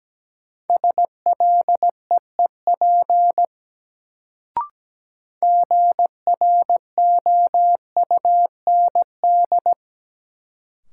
Morse code, heard as a keyed tone, about 17 wpm, 700 Hz. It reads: SLEEP GROUND
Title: Sentences from Top 500 Words - New Word Drill - 17wpm